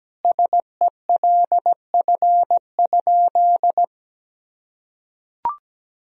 Morse code, heard as a keyed tone, about 17 wpm, 700 Hz. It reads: SELF?
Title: Sentences from Top 1000 Words - Review Part 1 - 17wpm